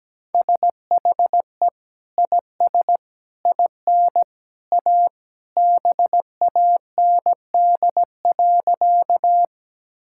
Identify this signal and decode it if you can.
Morse code, heard as a keyed tone, about 17 wpm, 700 Hz. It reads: SHE IS IN A BAND.
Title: Sentences from Top 1600 Words 17wpm